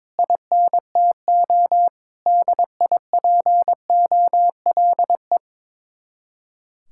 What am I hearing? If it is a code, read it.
Morse code, heard as a keyed tone, about 22 wpm, 700 Hz. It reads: INTO DIPOLE
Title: Speed-Racing Common QSO Phrases - 22wpm